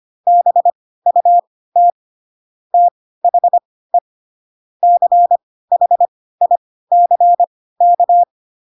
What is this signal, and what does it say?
Morse code, heard as a keyed tone, about 25 wpm, spaced as for 16 wpm, 700 Hz. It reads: BUT THE CHICK
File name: Pinocchio / Pinocchio ch7 10 25 ik1hge